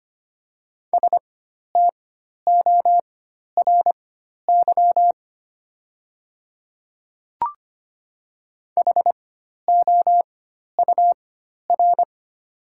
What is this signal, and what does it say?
Morse code, heard as a keyed tone, about 25 wpm, spaced as for 12 wpm, 700 Hz. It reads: STORY HOUR
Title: 4x Character Spacing - Top 500 Words - 25wpm